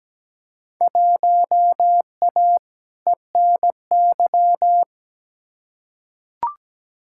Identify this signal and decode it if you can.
Morse code, heard as a keyed tone, about 17 wpm, 700 Hz. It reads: 1A ENY
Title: Field Day Exchanges - 17wpm